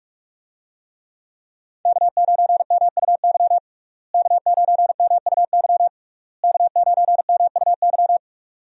Morse code, heard as a keyed tone, about 46 wpm, 700 Hz. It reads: K9MUY K9MUY K9MUY
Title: USA Call Signs - 5 Characters - Repeated 3x - 45wpm